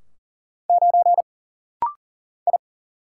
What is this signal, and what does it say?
Morse code, heard as a keyed tone, about 40 wpm, 700 Hz. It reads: 9 I
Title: Single Letter-Number - Rapid-Fire 40wpm